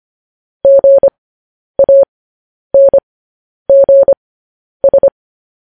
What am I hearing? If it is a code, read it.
Morse code, heard as a keyed tone, about 25 wpm, spaced as for 10 wpm, 550 Hz. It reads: GANGS
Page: news headlines in morse code at 10 wpm